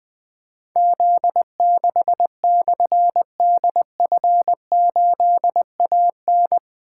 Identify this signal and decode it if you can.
Morse code, heard as a keyed tone, about 20 wpm, 700 Hz. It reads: Z6/DF8AN
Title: Difficult Call Signs 20wpm